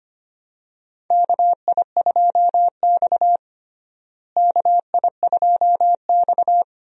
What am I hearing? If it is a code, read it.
Morse code, heard as a keyed tone, about 25 wpm, 700 Hz. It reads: KI2X KI2X
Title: USA Call Signs - 4 Characters - Repeated 3x - 25wpm